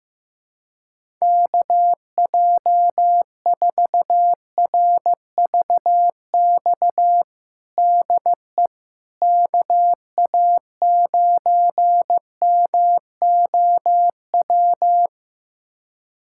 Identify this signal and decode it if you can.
Morse code, heard as a keyed tone, about 15 wpm, 700 Hz. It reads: KJ4RVX DE KA9MOW
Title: USA Callsign DE Callsign - 6 Characters - 15wpm